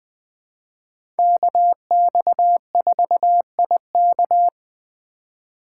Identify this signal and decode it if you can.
Morse code, heard as a keyed tone, about 20 wpm, 700 Hz. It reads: KX4IK
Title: USA Call Signs - 4x Character Spacing - 20wpm